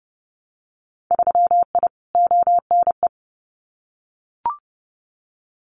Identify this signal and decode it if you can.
Morse code, heard as a keyed tone, about 30 wpm, 700 Hz. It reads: 3I ONE
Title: Winter Field Day Exchanges Encore 30wpm